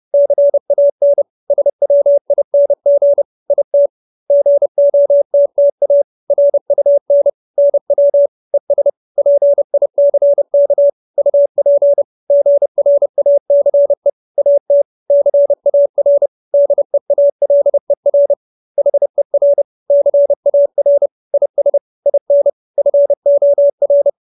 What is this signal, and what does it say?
Morse code, heard as a keyed tone, about 30 wpm, 570 Hz. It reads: CAN SWING IT GOTTA RUN NW ES PICK UP GRACE AT CAR DEALER HER CAR IS IN FOR